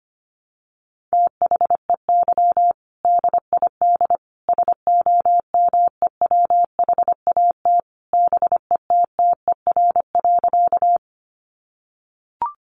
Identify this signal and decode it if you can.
Morse code, heard as a keyed tone, about 25 wpm, 700 Hz. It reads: THEY DID SOMEWHAT BETTER.
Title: Sentences from Top 1300 Words 25wpm